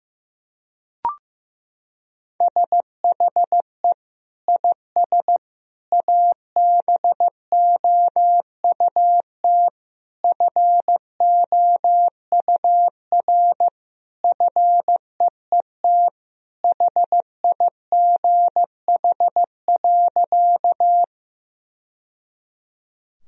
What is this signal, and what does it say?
Morse code, heard as a keyed tone, about 15 wpm, 700 Hz. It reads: SHE IS ABOUT FOUR FEET HIGH.
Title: Sentences from Top 1000 Words - Review Part 1 - 15wpm